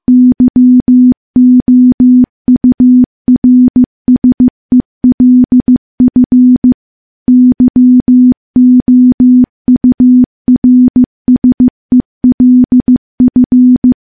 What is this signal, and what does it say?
Morse code, heard as a keyed tone, about 15 wpm, 250 Hz. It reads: YOURSELF YOURSELF
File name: morse / morse-training-15WPM